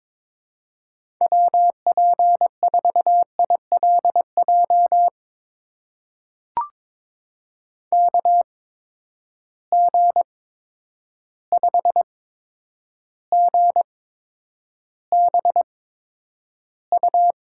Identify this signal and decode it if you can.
Morse code, heard as a keyed tone, about 22 wpm, spaced as for 9 wpm, 700 Hz. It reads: WP4ILJ KG5GBU
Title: USA Call Signs - 6 Characters - 8x Character Spacing - 22wpm